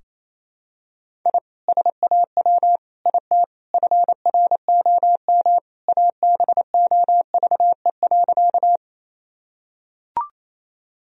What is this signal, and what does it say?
Morse code, heard as a keyed tone, about 28 wpm, 700 Hz. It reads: I SAW IT FROM ABOVE.
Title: Sentences from Top 1000 Words - Review Part 2 - 28wpm